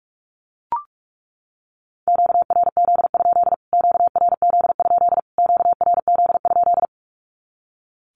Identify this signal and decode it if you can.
Morse code, heard as a keyed tone, about 44 wpm, 700 Hz. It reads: QRZ? QRZ? QRZ?
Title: Q-Codes Repeated 3x 45wpm